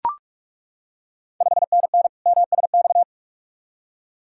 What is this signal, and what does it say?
Morse code, heard as a keyed tone, about 41 wpm, 690 Hz. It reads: 5NN MSX
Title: Ontario QSO Party Exchange 45wpm